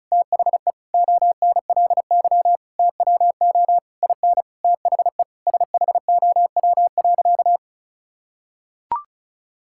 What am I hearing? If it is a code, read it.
Morse code, heard as a keyed tone, about 36 wpm, 700 Hz. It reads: THE ONLY TWO IN THE SHOW.
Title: Sentences from Top 1000 Words - Review Part 6 - 35wpm